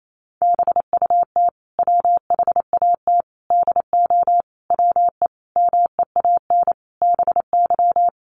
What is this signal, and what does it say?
Morse code, heard as a keyed tone, about 28 wpm, 700 Hz. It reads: BUT WHAT DO WE MEAN BY
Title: Sentences from Top 1100 Words 28wpm